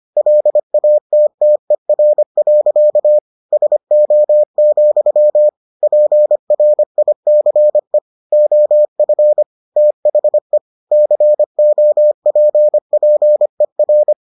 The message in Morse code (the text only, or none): LATTER. SO, PRICE OF THE COPPER